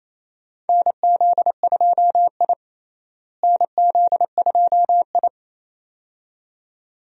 Morse code, heard as a keyed tone, about 27 wpm, 700 Hz. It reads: NZ2I NZ2I
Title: USA Call Signs - 4 Characters - Repeated 2x - 28wpm